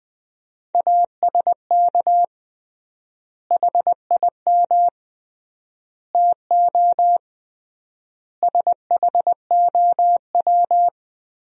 Morse code, heard as a keyed tone, about 20 wpm, 700 Hz. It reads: ASK HIM TO SHOW
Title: 3x Word-Spacing - Sentences from Top 200 Words - 20wpm